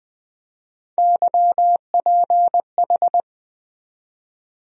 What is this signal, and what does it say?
Morse code, heard as a keyed tone, about 20 wpm, 700 Hz. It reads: YPH
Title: USA Call Sign Postfixes - 3 Letters 20wpm